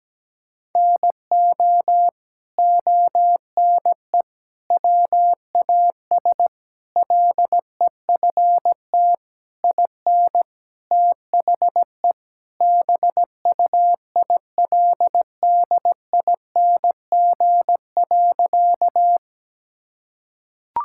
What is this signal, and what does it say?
Morse code, heard as a keyed tone, about 17 wpm, 700 Hz. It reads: NO ONE WAS LEFT IN THE BUILDING.
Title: Sentences from Top 1000 Words - Review Part 5 - 17wpm